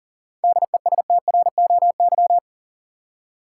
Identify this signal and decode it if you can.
Morse code, heard as a keyed tone, about 40 wpm, 700 Hz. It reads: DESTROY
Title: Seven Letter Words - 40wpm